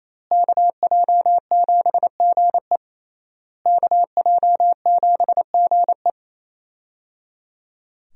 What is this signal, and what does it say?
Morse code, heard as a keyed tone, about 28 wpm, 700 Hz. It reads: KJ7GE KJ7GE
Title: USA Call Signs - 5 Characters - Repeated 3x - 28wpm